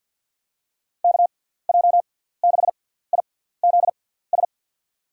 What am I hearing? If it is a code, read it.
Morse code, heard as a keyed tone, about 47 wpm, spaced as for 17 wpm, 700 Hz. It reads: KJ6IZS
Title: USA Call Signs - 6 Characters - 6x Character Spacing - 50wpm